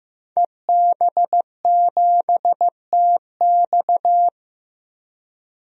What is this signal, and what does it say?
Morse code, heard as a keyed tone, about 15 wpm, 700 Hz. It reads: EB7TX
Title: DXCC-100 Call Signs - 5 Characters - Repeated 2x - 15wpm